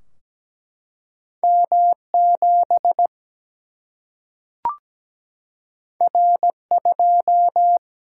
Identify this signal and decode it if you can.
Morse code, heard as a keyed tone, about 17 wpm, 700 Hz. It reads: M 7 R 2
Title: Single-Letter Single-Number 17wpm